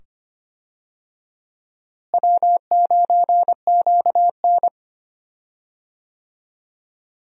Morse code, heard as a keyed tone, about 25 wpm, 700 Hz. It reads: W9QN
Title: USA Call Signs - 4 Characters - Repeated 2x - 25wpm